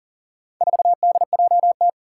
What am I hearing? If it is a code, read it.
Morse code, heard as a keyed tone, about 40 wpm, 700 Hz. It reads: 4DJT
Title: Four Challenging Letter-Number Combinations 40wpm